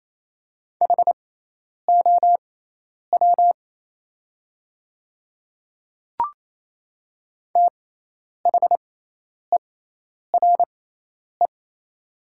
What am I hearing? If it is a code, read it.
Morse code, heard as a keyed tone, about 27 wpm, spaced as for 10 wpm, 700 Hz. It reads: HOW THERE
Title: 6x Character Spacing - Top 100 Words - 28wpm